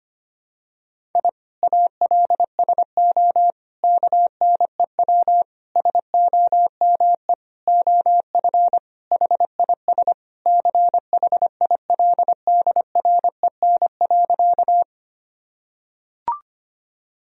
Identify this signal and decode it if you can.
Morse code, heard as a keyed tone, about 25 wpm, 700 Hz. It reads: I ALSO KNEW SOME OF HIS CHILDREN.
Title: Sentences from Top 1000 Words - Review Part 1 - 25wpm